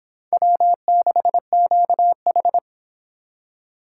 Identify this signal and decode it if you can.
Morse code, heard as a keyed tone, about 25 wpm, 700 Hz. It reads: W6QH